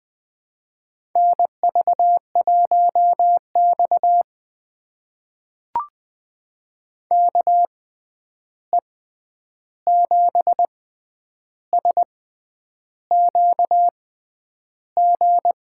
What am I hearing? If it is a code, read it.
Morse code, heard as a keyed tone, about 20 wpm, spaced as for 9 wpm, 700 Hz. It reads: NV1X KE7SQG
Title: USA Call Signs - 6x Character Spacing - 20wpm